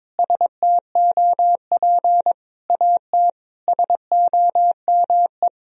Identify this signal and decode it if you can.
Morse code, heard as a keyed tone, about 22 wpm, 700 Hz. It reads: STOP AT SOME
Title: Sentences from Top 1000 Words - Review Part 1 - 22wpm